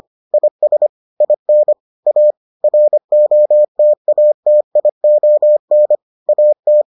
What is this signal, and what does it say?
Morse code, heard as a keyed tone, about 25 wpm, 600 Hz. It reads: IS IN A ROTATION AT